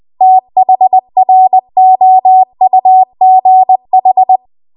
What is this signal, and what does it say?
Morse code, heard as a keyed tone, about 20 wpm, 750 Hz. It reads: THROUGH
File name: Archive / 20 WPM / 20WPM